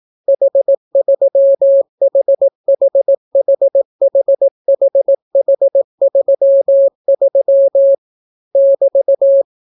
Morse code, heard as a keyed tone, about 18 wpm, 550 Hz. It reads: H3HHHHHH33 <BT>